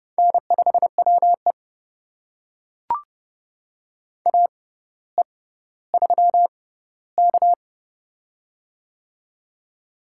Morse code, heard as a keyed tone, about 30 wpm, spaced as for 15 wpm, 700 Hz. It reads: N5WE AE3K
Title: USA Call Signs - 4 Characters - 6x Character Spacing - 30wpm